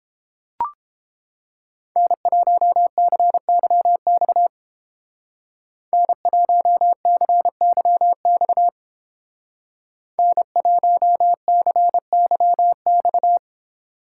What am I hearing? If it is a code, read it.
Morse code, heard as a keyed tone, about 29 wpm, 700 Hz. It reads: N1CYX N1CYX N1CYX